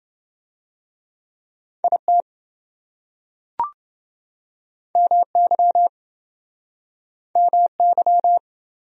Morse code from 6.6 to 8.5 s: MY

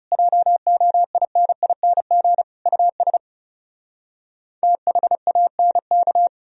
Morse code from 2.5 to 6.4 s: US THANK

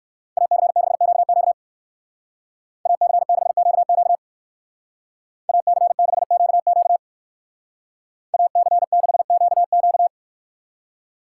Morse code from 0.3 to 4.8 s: AC6QQ AC6QQ

